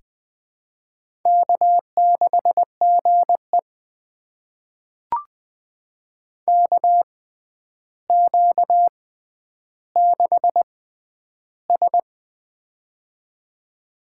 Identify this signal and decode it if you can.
Morse code, heard as a keyed tone, about 20 wpm, spaced as for 10 wpm, 700 Hz. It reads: K6GE KQ6S
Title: USA Call Signs - 6x Character Spacing - 20wpm